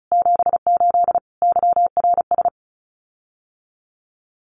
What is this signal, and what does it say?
Morse code, heard as a keyed tone, about 34 wpm, 700 Hz. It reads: 78 YRS
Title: Common QSO Phrases Encore 35wpm